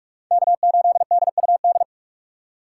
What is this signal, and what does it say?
Morse code, heard as a keyed tone, about 46 wpm, 700 Hz. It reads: K8DUD
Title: USA Call Signs - 5 Characters - 2x Character Spacing - 45wpm